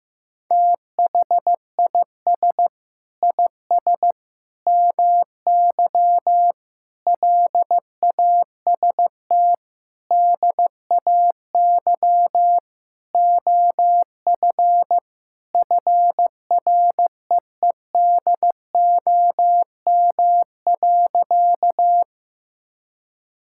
Morse code, heard as a keyed tone, about 15 wpm, 700 Hz. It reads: THIS IS MY LAST DAY OF FREEDOM.
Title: Sentences from Top 1000 Words - Review Part 5 - 15wpm